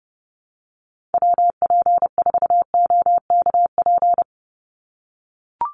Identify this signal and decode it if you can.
Morse code, heard as a keyed tone, about 30 wpm, 700 Hz. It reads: WP4OKP
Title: USA Call Signs 30wpm